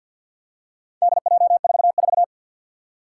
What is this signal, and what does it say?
Morse code, heard as a keyed tone, about 50 wpm, 700 Hz. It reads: DJ44
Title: Four Challenging Letter-Number Combinations 50wpm